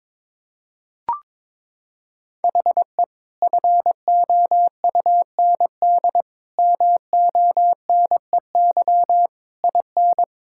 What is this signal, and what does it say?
Morse code, heard as a keyed tone, about 22 wpm, 700 Hz. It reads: HE FOUND MONEY IN
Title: Sentences from Top 1000 Words - Review Part 6 - 22wpm